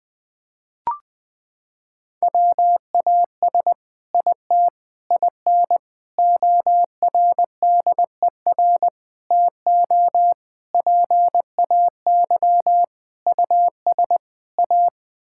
Morse code, heard as a keyed tone, about 20 wpm, 700 Hz. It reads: WAS IT IN ORDER TO PAY US A
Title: Sentences from Top 1000 Words - Review Part 6 - 20wpm